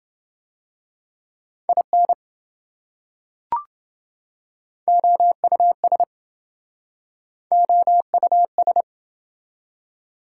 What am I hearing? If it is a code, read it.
Morse code, heard as a keyed tone, about 27 wpm, 700 Hz. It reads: IN OUS OUS